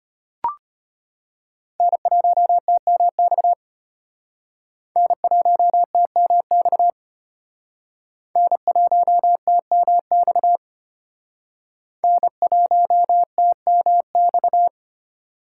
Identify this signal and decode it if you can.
Morse code, heard as a keyed tone, about 32 wpm, 700 Hz. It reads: N1TMX N1TMX N1TMX N1TMX